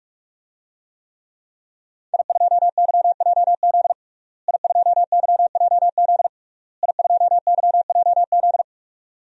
Morse code, heard as a keyed tone, about 46 wpm, 700 Hz. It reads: I2YJZ I2YJZ I2YJZ
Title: DXCC-100 Call Signs - 5 Characters - Repeated 3x - 45wpm